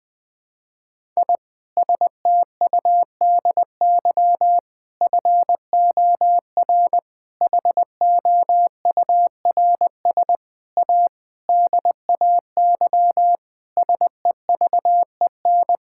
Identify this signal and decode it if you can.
Morse code, heard as a keyed tone, about 20 wpm, 700 Hz. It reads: I STUDY FOR HOURS A DAY SEVEN
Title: Sentences with 7-Word Phrases - 20wpm